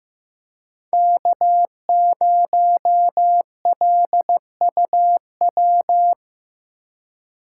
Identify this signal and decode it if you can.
Morse code, heard as a keyed tone, about 15 wpm, 700 Hz. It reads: K0LUW
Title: K1USN SST QSO Practice - 15wpm